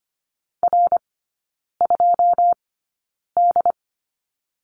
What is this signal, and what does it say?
Morse code, heard as a keyed tone, about 25 wpm, spaced as for 9 wpm, 700 Hz. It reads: R2D
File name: course / 25:10wpm / 093 Lesson 16e - Letter c - Three Characters 25wpm